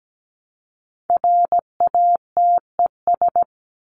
Morse code, heard as a keyed tone, about 17 wpm, 700 Hz. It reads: RATES
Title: ES:S Suffix - Top 100 Words with Base Words 17wpm